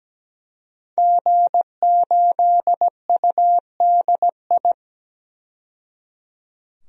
Morse code, heard as a keyed tone, about 17 wpm, 700 Hz. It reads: G8UDI
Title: DXCC-100 Call Signs - 5 Characters - 17wpm